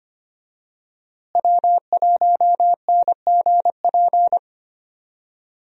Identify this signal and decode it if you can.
Morse code, heard as a keyed tone, about 25 wpm, 700 Hz. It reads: W1NGP
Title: USA Call Signs - 5 Characters - Repeated 3x - 25wpm